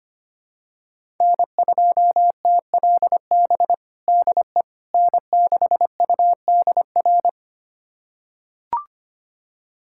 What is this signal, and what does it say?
Morse code, heard as a keyed tone, about 25 wpm, 700 Hz. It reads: N2TLB DE N6UDR
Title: USA Callsign DE Callsign - 5 Characters - 25wpm